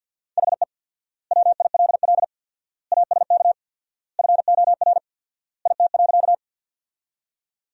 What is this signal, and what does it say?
Morse code, heard as a keyed tone, about 50 wpm, 700 Hz. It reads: HE WILL ASK FOR IT.
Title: 4x Word-Spacing - Sentences from Top 200 Words - 50wpm